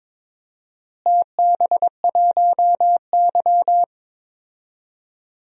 Morse code, heard as a keyed tone, about 22 wpm, 700 Hz. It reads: TB1Y
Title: DXCC-100 Call Signs - 4 Characters - Repeated 3x - 22wpm